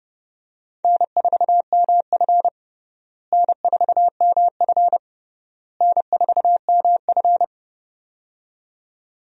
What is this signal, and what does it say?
Morse code, heard as a keyed tone, about 31 wpm, 700 Hz. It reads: N4MF N4MF N4MF
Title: USA Call Signs - 4 Characters - Repeated 3x - 30wpm